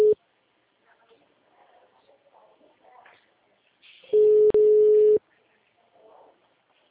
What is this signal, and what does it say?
Ringback tone (Europe, 425 Hz).
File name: out- -2030- - - .9556